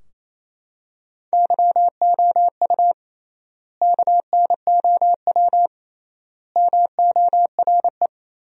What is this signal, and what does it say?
Morse code, heard as a keyed tone, about 28 wpm, 700 Hz. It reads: YOU KNOW MORE
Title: 3x Word-Spacing - Sentences from Top 100 Words - 28wpm